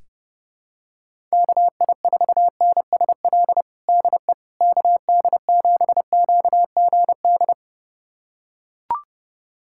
Morse code, heard as a keyed tone, about 32 wpm, 700 Hz. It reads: KI4NSL DE KD7QGD
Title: USA Callsign DE Callsign - 6 Characters - 30wpm